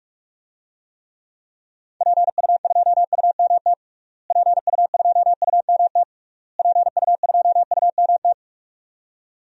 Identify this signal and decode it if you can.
Morse code, heard as a keyed tone, about 41 wpm, 700 Hz. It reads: PU2UMT PU2UMT PU2UMT
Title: DXCC-100 Call Signs - 6 Characters - Repeated 3x - 45wpm